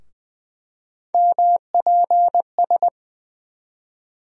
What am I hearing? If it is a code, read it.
Morse code, heard as a keyed tone, about 20 wpm, 700 Hz. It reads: MPS
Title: USA Call Sign Postfixes - 3 Letters 20wpm